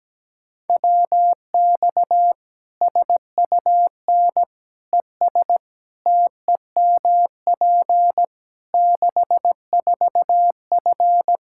Morse code, heard as a keyed tone, about 17 wpm, 700 Hz. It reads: WX SUN ES TEMP 64F
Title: Practice QSO - Exchange 2 - Info - 17wpm